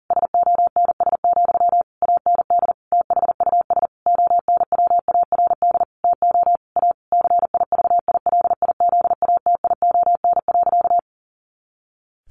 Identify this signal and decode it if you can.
Morse code, heard as a keyed tone, about 40 wpm, 700 Hz. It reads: SONS, AND THUS ONWARD TO A CIVILIZATION.
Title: A Sound Of Thunder 40wpm